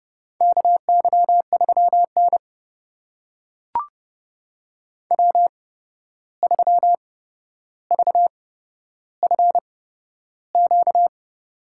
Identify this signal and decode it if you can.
Morse code, heard as a keyed tone, about 30 wpm, spaced as for 12 wpm, 700 Hz. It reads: KY3N W3VFQ
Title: USA Call Signs - 8x Character Spacing - 30wpm